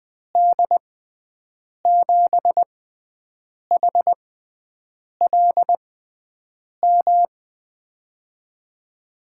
Morse code, heard as a keyed tone, about 20 wpm, spaced as for 7 wpm, 700 Hz. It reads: D7HLM
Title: USA Call Signs - 6 Characters - 6x Character Spacing - 20wpm